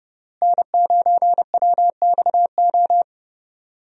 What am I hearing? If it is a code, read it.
Morse code, heard as a keyed tone, about 30 wpm, 700 Hz. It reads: N9WXO
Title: USA Call Signs - 5 Characters - Repeated 3x - 30wpm